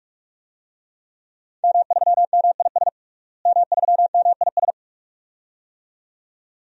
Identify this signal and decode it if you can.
Morse code, heard as a keyed tone, about 46 wpm, 700 Hz. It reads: M3MIS M3MIS
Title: DXCC-100 Call Signs - 5 Characters - Repeated 2x - 45wpm